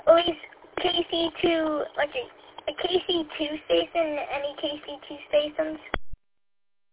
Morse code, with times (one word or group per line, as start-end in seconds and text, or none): none